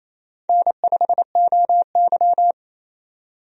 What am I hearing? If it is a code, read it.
Morse code, heard as a keyed tone, about 28 wpm, 700 Hz. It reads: N5OY
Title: USA Call Signs - 4 Characters - 28wpm